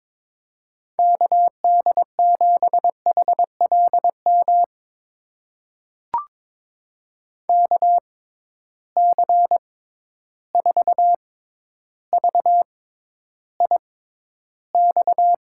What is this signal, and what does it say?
Morse code, heard as a keyed tone, about 22 wpm, spaced as for 11 wpm, 700 Hz. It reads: KD7HLM KC4VIX
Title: USA Call Signs - 6 Characters - 6x Character Spacing - 22wpm